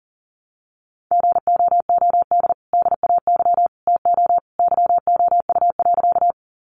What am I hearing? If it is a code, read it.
Morse code, heard as a keyed tone, about 38 wpm, 700 Hz. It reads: GOOD DAY TO YOU.
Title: Sentences from Top 200 Words 40wpm